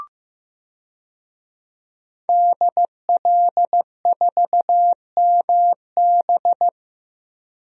Morse code, heard as a keyed tone, about 15 wpm, 700 Hz. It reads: DL4MB
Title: DXCC-100 Call Signs - 5 Characters - 15wpm